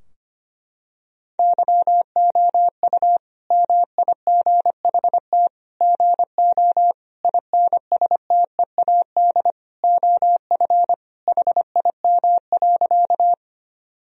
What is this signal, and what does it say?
Morse code, heard as a keyed tone, about 25 wpm, 700 Hz. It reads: YOU MIGHT GO INSTEAD OF HIM.
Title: Sentences from Top 1000 Words - Review Part 4 - 25wpm